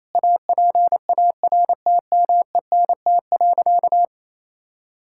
Morse code, heard as a keyed tone, about 28 wpm, 700 Hz. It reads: APARTMENT.
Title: Sentences from Top 1000 Words - Review Part 6 - 28wpm